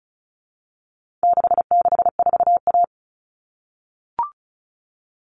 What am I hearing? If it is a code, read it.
Morse code, heard as a keyed tone, about 34 wpm, 700 Hz. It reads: 664A
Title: All Cut Numbers - 4-Digit 35wpm